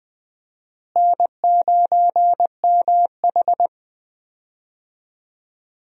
Morse code, heard as a keyed tone, about 20 wpm, 700 Hz. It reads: N9MH
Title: USA Call Signs - 4 Characters - Repeated 3x - 20wpm